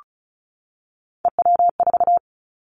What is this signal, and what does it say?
Morse code, heard as a keyed tone, about 36 wpm, 700 Hz. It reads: EW4
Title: Callsign Prefixes for Top 100 DXCC - 35wpm